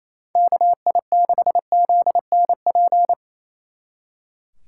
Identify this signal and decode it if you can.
Morse code, heard as a keyed tone, about 34 wpm, 700 Hz. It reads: KI6ZNP
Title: Speed Racing - USA 2x3 Call Signs 28wpm